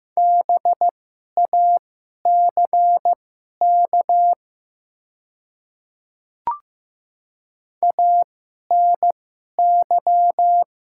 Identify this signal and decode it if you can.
Morse code, heard as a keyed tone, about 15 wpm, spaced as for 11 wpm, 700 Hz. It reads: BACK ANY
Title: 2x Character Spacing - Top 200 Words - 15wpm